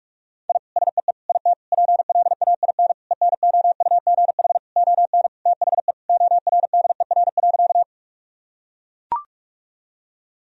Morse code, heard as a keyed tone, about 42 wpm, 700 Hz. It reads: I SEE IT PLAIN ENOUGH ON THE ORDER.
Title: Sentences from Top 1000 Words - Review Part 2 - 45wpm